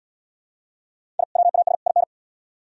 Morse code, heard as a keyed tone, about 71 wpm, 700 Hz. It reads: I WAS IN